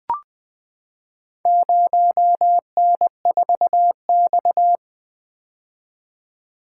Morse code, heard as a keyed tone, about 20 wpm, 700 Hz. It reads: 0N4X
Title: Four Letter-Number Combinations - 20wpm